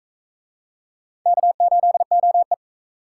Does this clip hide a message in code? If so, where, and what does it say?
Yes: Morse code from 1.3 to 2.5 s, heard as a keyed tone, about 40 wpm, 700 Hz. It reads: K8OE